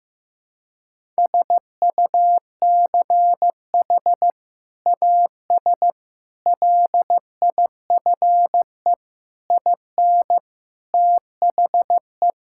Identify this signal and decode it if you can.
Morse code, heard as a keyed tone, about 16 wpm, 700 Hz. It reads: SUCH AS LIFE IN THE
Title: Sentences from Top 1000 Words - Review Part 5 - 15wpm